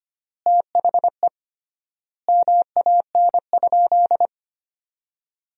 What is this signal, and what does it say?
Morse code, heard as a keyed tone, about 25 wpm, 700 Hz. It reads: THE MAN?
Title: 3x Word-Spacing - Sentences from Top 200 Words - 25wpm